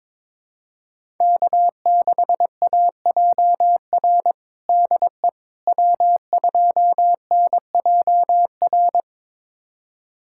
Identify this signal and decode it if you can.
Morse code, heard as a keyed tone, about 22 wpm, 700 Hz. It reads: K6AJR DE W2NJR
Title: USA Callsign DE Callsign - 5 Characters - 22wpm